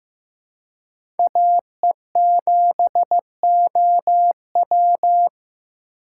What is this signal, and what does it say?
Morse code, heard as a keyed tone, about 15 wpm, 700 Hz. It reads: AE7OW
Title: USA Call Signs - 5 Characters - 3x Character Spacing - 15wpm